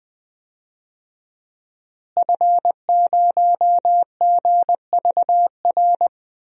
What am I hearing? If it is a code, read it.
Morse code, heard as a keyed tone, about 20 wpm, 700 Hz. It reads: F0GVR
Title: DXCC-100 Call Signs - 5 Characters - Repeated 3x - 20wpm